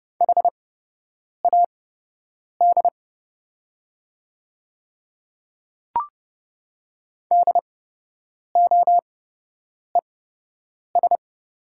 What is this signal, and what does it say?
Morse code, heard as a keyed tone, about 30 wpm, spaced as for 8 wpm, 700 Hz. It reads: HAD DOES
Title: 8x Character Spacing - Top 200 Words - 30wpm